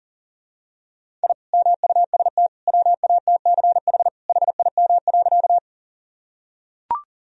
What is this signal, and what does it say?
Morse code, heard as a keyed tone, about 40 wpm, 700 Hz. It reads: I MUST WATCH HIM.
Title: Sentences from Top 1000 Words - Review Part 1 - 40wpm